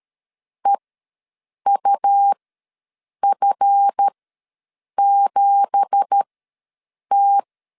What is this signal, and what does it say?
Morse code, heard as a keyed tone, about 12 wpm, spaced as for 7 wpm, 800 Hz. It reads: EUF7T
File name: LEZIONI / 9 2867 ws 12wpm 3431